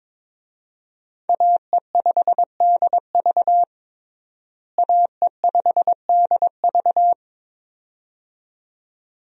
Morse code, heard as a keyed tone, about 22 wpm, 700 Hz. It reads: AE5DV AE5DV
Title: USA Call Signs - 5 Characters - Repeated 2x - 22wpm